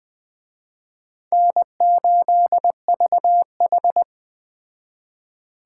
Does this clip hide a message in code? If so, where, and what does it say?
Yes: Morse code from 1.3 to 4.0 s, heard as a keyed tone, about 21 wpm, 700 Hz. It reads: N8VH